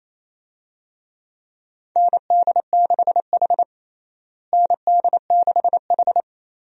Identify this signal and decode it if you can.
Morse code, heard as a keyed tone, about 28 wpm, 700 Hz. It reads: ND6H ND6H
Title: USA Call Signs - 4 Characters - Repeated 2x - 28wpm